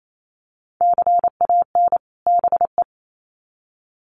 Morse code, heard as a keyed tone, about 28 wpm, 700 Hz. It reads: CAN BE
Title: Sets of 2 Words 28wpm